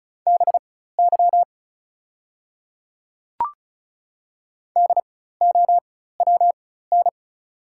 Morse code, heard as a keyed tone, about 36 wpm, spaced as for 16 wpm, 690 Hz. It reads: BY DOWN
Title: 4x Character Spacing - Top 100 Words - 35wpm